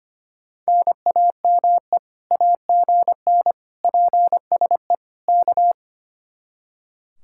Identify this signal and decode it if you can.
Morse code, heard as a keyed tone, about 32 wpm, 700 Hz. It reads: NAME AGN PSE K